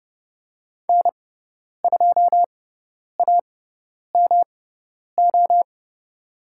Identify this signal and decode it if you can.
Morse code, heard as a keyed tone, about 30 wpm, spaced as for 10 wpm, 700 Hz. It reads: N2AMO
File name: Lesson 18g - Letter m - Call Signs 30wpm